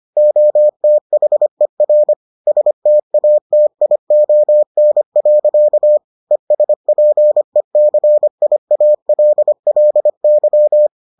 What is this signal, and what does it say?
Morse code, heard as a keyed tone, about 25 wpm, 600 Hz. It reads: OTHER STATION. ESPECIALLY